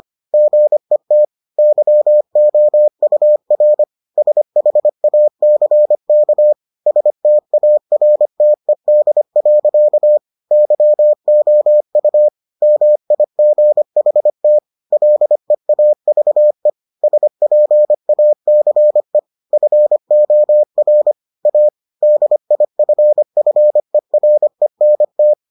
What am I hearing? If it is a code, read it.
Morse code, heard as a keyed tone, about 25 wpm, 600 Hz. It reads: GET YOUR SHACK STARTED. YOU MIGHT LEAVE SPACE FOR A DIFFERENT